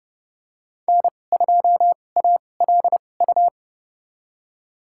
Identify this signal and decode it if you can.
Morse code, heard as a keyed tone, about 30 wpm, spaced as for 22 wpm, 700 Hz. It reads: N2ALU
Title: USA Call Signs - 5 Characters - 2x Character Spacing - 30wpm